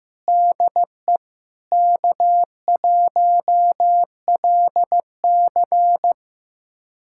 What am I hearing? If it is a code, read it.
Morse code, heard as a keyed tone, about 15 wpm, 700 Hz. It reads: DE K1LC
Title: USA Callsign DE Callsign - 4 Characters - 15wpm